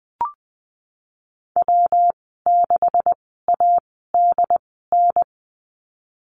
Morse code, heard as a keyed tone, about 19 wpm, spaced as for 14 wpm, 700 Hz. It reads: W6ADN
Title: 179 Lesson 29g - Slash - Call Signs 20wpm